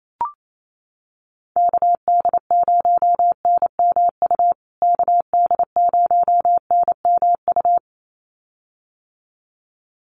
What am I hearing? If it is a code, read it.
Morse code, heard as a keyed tone, about 28 wpm, 700 Hz. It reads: KD0NMU KD0NMU
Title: USA Call Signs Repeated 2x 28wpm